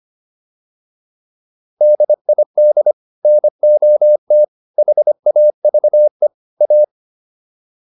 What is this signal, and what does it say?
Morse code, heard as a keyed tone, about 25 wpm, 600 Hz. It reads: DID NOT HAVE A